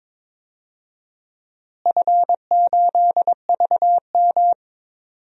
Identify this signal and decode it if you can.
Morse code, heard as a keyed tone, about 22 wpm, 700 Hz. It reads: F8VM
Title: DXCC-100 Call Signs - 4 Characters - Repeated 3x - 22wpm